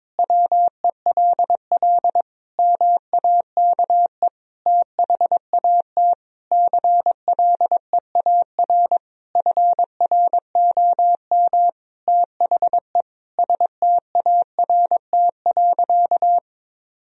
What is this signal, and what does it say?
Morse code, heard as a keyed tone, about 22 wpm, 700 Hz. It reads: WELL MAKE THAT CLEAR FROM THE START.
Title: Sentences from Top 1000 Words - Review Part 1 - 22wpm